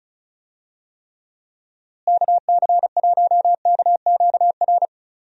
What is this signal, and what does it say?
Morse code, heard as a keyed tone, about 36 wpm, 700 Hz. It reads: KC1KQR
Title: USA Call Signs - 6 Characters - Repeated 2x - 35wpm